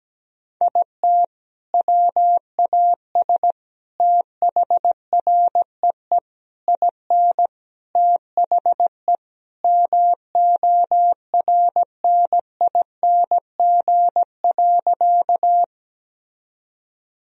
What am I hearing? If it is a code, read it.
Morse code, heard as a keyed tone, about 17 wpm, 700 Hz. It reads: IT WAS THREE IN THE MORNING.
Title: Sentences from Top 1000 Words - Review Part 1 - 17wpm